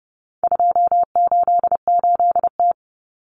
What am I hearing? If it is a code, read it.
Morse code, heard as a keyed tone, about 30 wpm, 700 Hz. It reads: 288T
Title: Common Cut Numbers - 4-Digit 30wpm